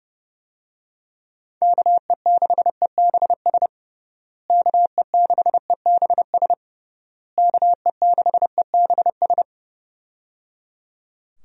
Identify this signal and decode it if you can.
Morse code, heard as a keyed tone, about 29 wpm, 700 Hz. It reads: KE6EBS KE6EBS KE6EBS
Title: USA Call Signs - 6 Characters - Repeated 3x - 30wpm